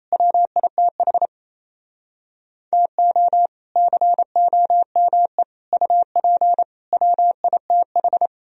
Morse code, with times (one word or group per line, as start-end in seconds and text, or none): 0.1-1.3 WITH
2.7-3.5 TO
3.8-5.4 COME
5.7-6.6 UP
6.9-8.3 WITH